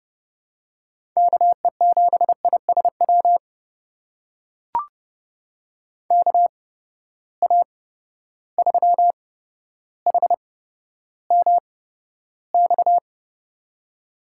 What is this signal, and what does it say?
Morse code, heard as a keyed tone, about 30 wpm, spaced as for 13 wpm, 700 Hz. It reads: KE7ISW KA3HMX
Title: USA Call Signs - 6 Characters - 8x Character Spacing - 30wpm